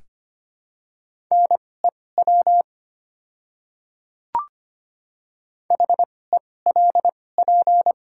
Morse code, heard as a keyed tone, about 25 wpm, spaced as for 18 wpm, 700 Hz. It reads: NEW HELP
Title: 2x Character Spacing - Top 300 Words - 25wpm